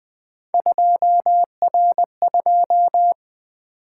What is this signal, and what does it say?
Morse code, heard as a keyed tone, about 20 wpm, 700 Hz. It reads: 2R2
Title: Three Letter-Number Combinations - 20wpm